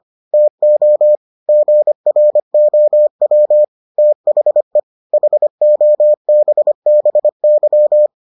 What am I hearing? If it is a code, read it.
Morse code, heard as a keyed tone, about 25 wpm, 600 Hz. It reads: TO GROW THE HOBBY